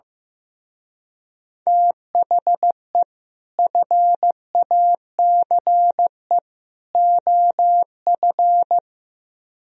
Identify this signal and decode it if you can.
Morse code, heard as a keyed tone, about 15 wpm, 700 Hz. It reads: THE FACE OF
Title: Speed-Racing - Sets of 3 Words 15wpm